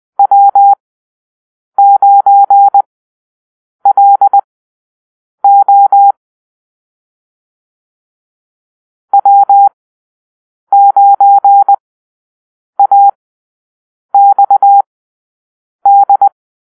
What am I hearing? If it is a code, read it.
Morse code, heard as a keyed tone, about 20 wpm, spaced as for 7 wpm, 800 Hz. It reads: W9LO W9AXD
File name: arrl / Call-Sign Drill-7